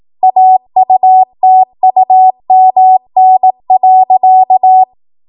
Morse code, heard as a keyed tone, about 18 wpm, 750 Hz. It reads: AUTUMN.
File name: Morse / 18WPM